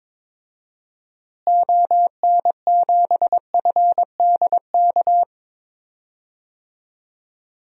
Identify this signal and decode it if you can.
Morse code, heard as a keyed tone, about 21 wpm, 700 Hz. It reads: ON7FDK
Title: DXCC-100 Call Signs - 6 Characters - Repeated 3x - 22wpm